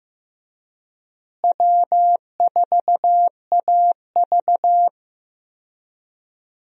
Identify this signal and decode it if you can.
Morse code, heard as a keyed tone, about 15 wpm, 700 Hz. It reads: W4AV
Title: USA Call Signs - 4 Characters - Repeated 3x - 15wpm